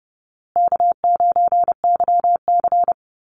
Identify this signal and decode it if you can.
Morse code, heard as a keyed tone, about 30 wpm, 700 Hz. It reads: K9YC
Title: Easy Call Signs Repeated 2x - 30wpm